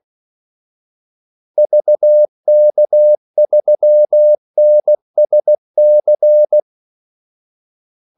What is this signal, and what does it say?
Morse code, heard as a keyed tone, about 16 wpm, 600 Hz. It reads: VK3NSC
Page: Kaldesignaler 16 wpm